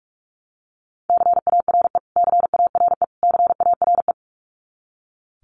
Morse code, heard as a keyed tone, about 44 wpm, 700 Hz. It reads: CARE CARE CARE
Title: Top 500 Words Repeated 45wpm